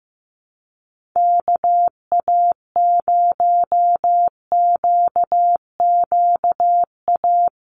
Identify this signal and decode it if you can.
Morse code, heard as a keyed tone, about 15 wpm, 700 Hz. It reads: KA0QQA
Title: USA Call Signs Repeated 3x 15wpm